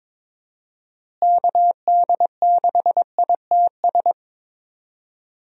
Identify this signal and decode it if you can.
Morse code, heard as a keyed tone, about 22 wpm, 700 Hz. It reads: KD6ITS
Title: USA Call Signs - 6 Characters - Repeated 2x - 22wpm